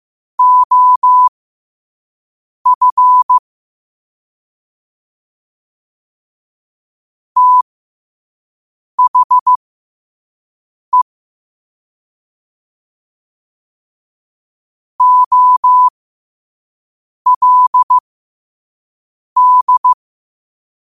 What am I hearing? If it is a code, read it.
Morse code, heard as a keyed tone, about 15 wpm, spaced as for 5 wpm, 1000 Hz. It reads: OF THE OLD
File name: qotdfiles / 5wpm